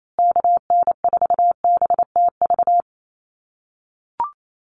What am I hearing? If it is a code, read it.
Morse code, heard as a keyed tone, about 28 wpm, 700 Hz. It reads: KN4BTV
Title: USA Call Signs Repeated 2x 28wpm